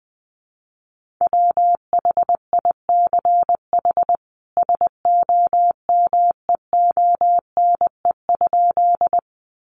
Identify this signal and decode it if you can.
Morse code, heard as a keyed tone, about 20 wpm, 700 Hz. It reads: WHICH SOMEONE?
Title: Sentences from Top 600 Words 20wpm